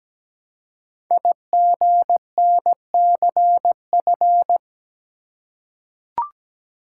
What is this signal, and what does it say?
Morse code, heard as a keyed tone, about 17 wpm, 700 Hz. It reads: IGNCF
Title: Code Groups - 5 Letters - 17wpm